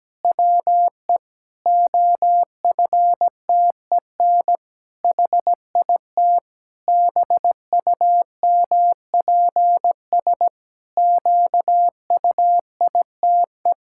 Morse code, heard as a keyed tone, about 17 wpm, 700 Hz. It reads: WE OFTEN HIT BUMPS QUITE
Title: Pangram Challenge v2 - 17wpm